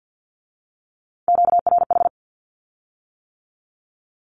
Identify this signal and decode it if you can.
Morse code, heard as a keyed tone, about 50 wpm, 700 Hz. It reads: QRH
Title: Q-Codes with Spoken Meaning - 50wpm